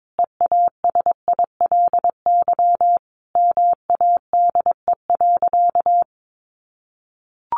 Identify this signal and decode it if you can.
Morse code, heard as a keyed tone, about 22 wpm, 700 Hz. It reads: EASILY MADE.
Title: Sentences from Top 1200 Words 22wpm